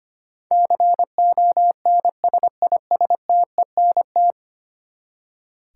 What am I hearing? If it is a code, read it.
Morse code, heard as a keyed tone, about 25 wpm, 700 Hz. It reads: CONSISTENT
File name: Ten Letter Words - 25wpm